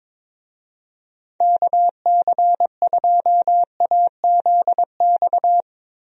Morse code, heard as a keyed tone, about 21 wpm, 700 Hz. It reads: KC2AZX
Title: USA Call Signs - 6 Characters - Repeated 3x - 22wpm